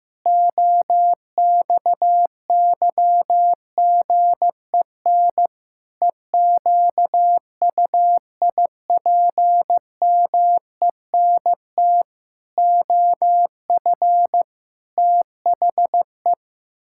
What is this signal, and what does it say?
Morse code, heard as a keyed tone, about 16 wpm, 700 Hz. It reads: OXYGEN EQUIPMENT OF THE
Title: Pangram Challenge v2 - 15wpm